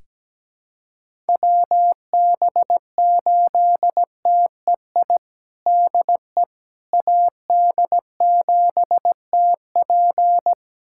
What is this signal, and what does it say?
Morse code, heard as a keyed tone, about 17 wpm, 700 Hz. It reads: WB8TEI DE AD7TP
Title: USA Callsign DE Callsign 17wpm